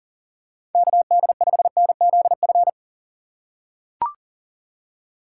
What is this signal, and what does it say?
Morse code, heard as a keyed tone, about 40 wpm, 700 Hz. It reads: KD5NZF
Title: USA Call Signs - 6 Characters - 4x Character Spacing - 40wpm